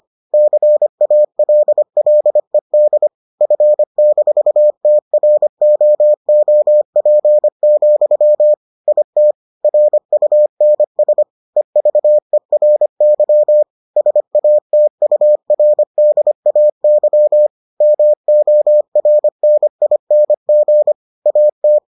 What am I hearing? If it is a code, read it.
Morse code, heard as a keyed tone, about 25 wpm, 600 Hz. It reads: CALLED F-TROOP, IT RUNS EVERY SATURDAY MORNING AT